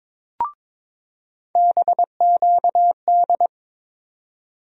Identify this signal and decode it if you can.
Morse code, heard as a keyed tone, about 22 wpm, 700 Hz. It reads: BQD
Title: USA Call Sign Postfixes - 3 Letters 22wpm